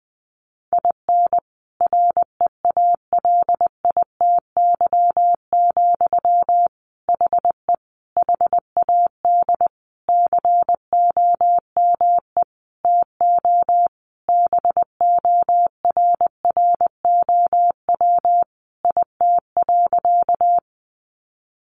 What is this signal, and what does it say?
Morse code, heard as a keyed tone, about 20 wpm, 700 Hz. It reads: IN REALITY, HE HAD COME TO BORROW IT.
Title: Sentences from Top 1500 Words 20wpm